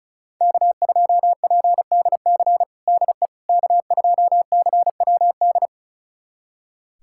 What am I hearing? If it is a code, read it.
Morse code, heard as a keyed tone, about 36 wpm, 700 Hz. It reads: K2PDC DE K2CWD
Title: USA Callsign DE Callsign - 5 Characters - 35wpm